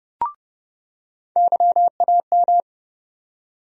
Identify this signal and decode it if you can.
Morse code, heard as a keyed tone, about 32 wpm, 700 Hz. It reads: YAM
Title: USA Call Sign Postfixes - 3 Letters 30wpm